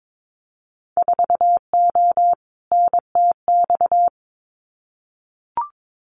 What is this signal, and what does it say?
Morse code, heard as a keyed tone, about 22 wpm, 700 Hz. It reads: 4O NTX
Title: Winter Field Day Exchanges 22wpm